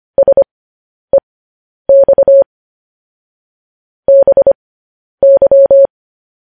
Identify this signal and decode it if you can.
Morse code, heard as a keyed tone, about 25 wpm, spaced as for 10 wpm, 550 Hz. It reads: SEX BY